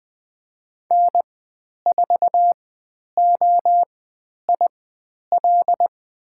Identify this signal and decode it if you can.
Morse code, heard as a keyed tone, about 20 wpm, spaced as for 10 wpm, 700 Hz. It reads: N4OIL
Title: 095 Lesson 16g - Letter c - Call Signs 20wpm